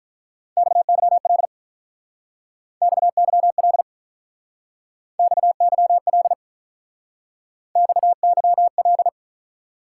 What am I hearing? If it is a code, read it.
Morse code, heard as a keyed tone, about 44 wpm, 700 Hz. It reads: XYL XYL XYL XYL